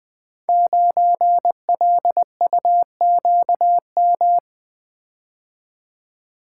Morse code, heard as a keyed tone, about 20 wpm, 700 Hz. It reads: 9LUQM
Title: Code Groups - 5 Letter:Numbers - 20wpm